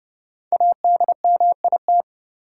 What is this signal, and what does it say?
Morse code, heard as a keyed tone, about 30 wpm, 700 Hz. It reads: ADMIT
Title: Five Letter Words - 30wpm